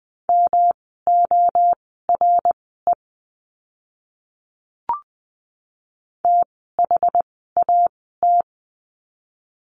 Morse code, heard as a keyed tone, about 20 wpm, spaced as for 14 wpm, 700 Hz. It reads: MORE THAT
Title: 206 Lesson 33f - Letter j - Words 20wpm